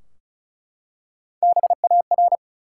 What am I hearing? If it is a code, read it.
Morse code, heard as a keyed tone, about 36 wpm, 700 Hz. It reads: BAR
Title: Three Letter Words - 35wpm